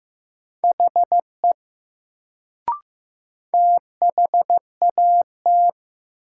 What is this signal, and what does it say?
Morse code, heard as a keyed tone, about 15 wpm, 700 Hz. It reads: HE THAT
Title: Sentences from Top 100 Words - New Word Drill - 15wpm